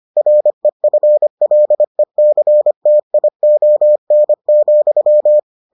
Morse code, heard as a keyed tone, about 25 wpm, 600 Hz. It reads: REFLECTION,